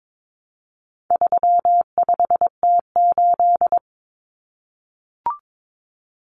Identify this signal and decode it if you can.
Morse code, heard as a keyed tone, about 22 wpm, 700 Hz. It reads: 35T8
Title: Common Cut Numbers - 4-Digit 22wpm